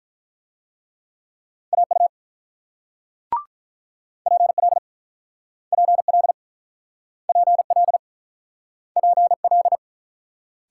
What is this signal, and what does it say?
Morse code, heard as a keyed tone, about 48 wpm, 700 Hz. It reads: AU PL PL PL PL